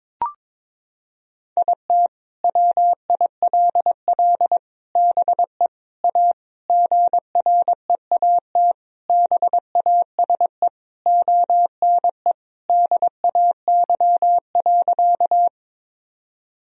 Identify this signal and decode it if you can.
Morse code, heard as a keyed tone, about 22 wpm, 700 Hz. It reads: IT WILL BE A GREAT BASE ONE DAY.
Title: Sentences from Top 1000 Words - Review Part 1 - 22wpm